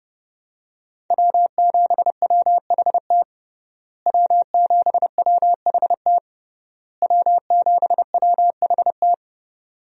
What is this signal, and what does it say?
Morse code, heard as a keyed tone, about 30 wpm, 700 Hz. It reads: W7WHT W7WHT W7WHT
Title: USA Call Signs - 5 Characters - Repeated 3x - 30wpm